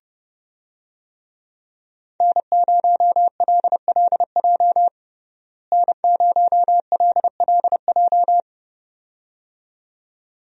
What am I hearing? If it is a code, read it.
Morse code, heard as a keyed tone, about 30 wpm, 700 Hz. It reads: N0LLJ N0LLJ
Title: USA Call Signs - 5 Characters - Repeated 2x - 30wpm